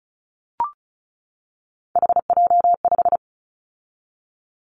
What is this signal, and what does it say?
Morse code, heard as a keyed tone, about 34 wpm, 700 Hz. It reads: HJ5
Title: Callsign Prefixes for Top 100 DXCC - 35wpm